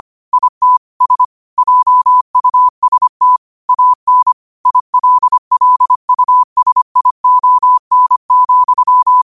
Morse code, heard as a keyed tone, about 25 wpm, 1000 Hz. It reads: IT S JUST AN ILLUSION,
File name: qotdfiles / 25wpm